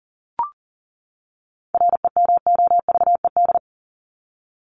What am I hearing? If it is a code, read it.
Morse code, heard as a keyed tone, about 40 wpm, 700 Hz. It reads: REMOVED
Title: ED Suffix - Top 100 Words with Base Words 40wpm